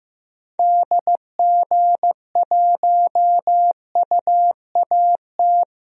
Morse code, heard as a keyed tone, about 15 wpm, 690 Hz. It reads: DG1UAT
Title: DXCC-100 Call Signs - 6 Characters - 15wpm